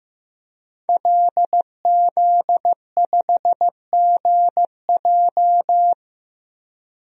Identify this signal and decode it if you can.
Morse code, heard as a keyed tone, about 15 wpm, 700 Hz. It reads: LZ5GJ
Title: DXCC-100 Call Signs - 5 Characters - 15wpm